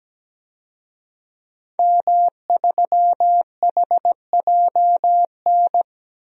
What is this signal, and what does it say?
Morse code, heard as a keyed tone, about 17 wpm, 700 Hz. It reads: M3HJN
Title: DXCC-100 Call Signs - 5 Characters - 17wpm